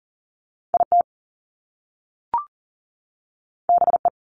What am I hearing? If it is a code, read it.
Morse code, heard as a keyed tone, about 40 wpm, 700 Hz. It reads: IT BE
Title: Top Two Letter Combinations in Words 40wpm